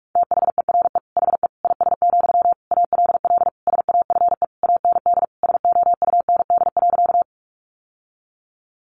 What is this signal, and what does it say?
Morse code, heard as a keyed tone, about 44 wpm, 700 Hz. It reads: THERE HE IS, ALL SAFE AND SOUND.
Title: Sentences from Top 900 Words 45wpm